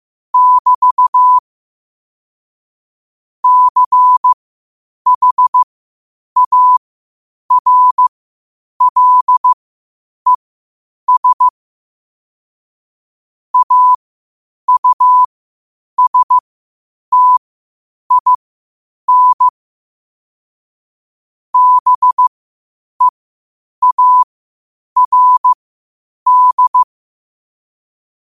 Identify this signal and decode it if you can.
Morse code, heard as a keyed tone, about 15 wpm, spaced as for 8 wpm, 1000 Hz. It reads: <BT> CHARLES AUSTIN BEARD
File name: qotdfiles / 8wpm